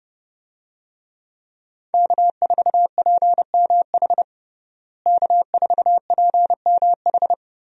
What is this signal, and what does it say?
Morse code, heard as a keyed tone, about 30 wpm, 700 Hz. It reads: K4PMH K4PMH
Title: USA Call Signs - 5 Characters - Repeated 3x - 30wpm